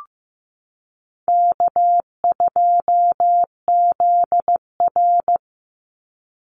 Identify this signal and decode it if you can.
Morse code, heard as a keyed tone, about 15 wpm, 700 Hz. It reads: K2ZR
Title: Easy Call Signs - 15wpm